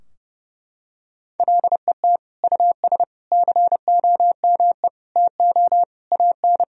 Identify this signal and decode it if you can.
Morse code, heard as a keyed tone, about 32 wpm, 700 Hz. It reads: LET US COME TO AN
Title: Sentences from Top 1000 Words - Review Part 3 - 30wpm